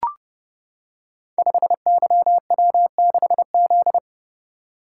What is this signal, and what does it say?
Morse code, heard as a keyed tone, about 28 wpm, 700 Hz. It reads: 5YW6Z
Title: Code Groups - 5 Letter:Numbers - 30wpm